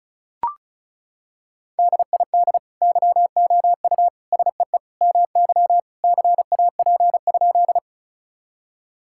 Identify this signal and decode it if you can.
Morse code, heard as a keyed tone, about 36 wpm, 700 Hz. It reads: DID YOU SEE MY CAP?
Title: Sentences from Top 1600 Words 35wpm